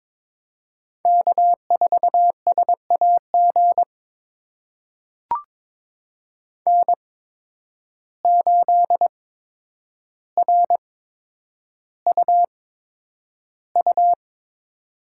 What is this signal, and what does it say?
Morse code, heard as a keyed tone, about 22 wpm, spaced as for 9 wpm, 700 Hz. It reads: K4SAG N8RUU
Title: USA Call Signs - 5 Characters - 8x Character Spacing - 22wpm